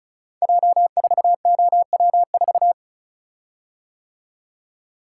Morse code, heard as a keyed tone, about 36 wpm, 700 Hz. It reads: J4OW4
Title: Code Groups - 5 Letter:Numbers - 35wpm